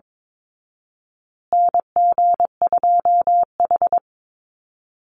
Morse code, heard as a keyed tone, about 22 wpm, 700 Hz. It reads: NG2H
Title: Call Signs Repeated 2x - 22wpm